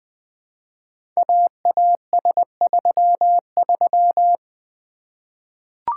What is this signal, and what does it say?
Morse code, heard as a keyed tone, about 20 wpm, 700 Hz. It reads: AAS33
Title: Code Groups - 5 Letter:Numbers - 20wpm